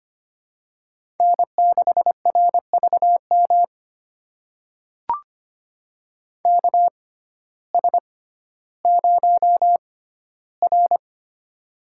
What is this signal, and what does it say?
Morse code, heard as a keyed tone, about 25 wpm, spaced as for 14 wpm, 700 Hz. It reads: N6RVM KS0R
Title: USA Call Signs - 6x Character Spacing - 25wpm